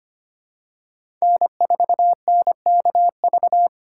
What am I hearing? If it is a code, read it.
Morse code, heard as a keyed tone, about 25 wpm, 700 Hz. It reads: N4NKV
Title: USA Call Signs - 4x Character Spacing - 25wpm